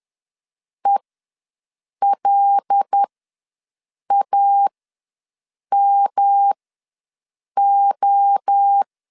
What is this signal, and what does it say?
Morse code, heard as a keyed tone, about 10 wpm, spaced as for 6 wpm, 800 Hz. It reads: ELAMO
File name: LEZIONI / 6 PKX0 / 6 PKX ws 10wpm 3706